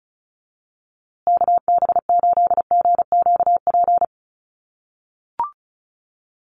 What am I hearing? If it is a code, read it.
Morse code, heard as a keyed tone, about 34 wpm, 700 Hz. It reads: KB8GQP
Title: USA Call Signs Repeated 3x 35wpm